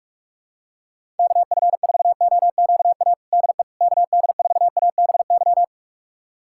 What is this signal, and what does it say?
Morse code, heard as a keyed tone, about 42 wpm, 700 Hz. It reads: KF4OQA DE KD4ABY
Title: USA Callsign DE Callsign 45wpm